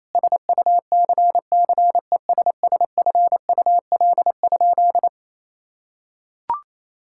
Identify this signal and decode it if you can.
Morse code, heard as a keyed tone, about 28 wpm, 700 Hz. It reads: SUCCESSFUL?
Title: Sentences from Top 1000 Words - Review Part 5 - 28wpm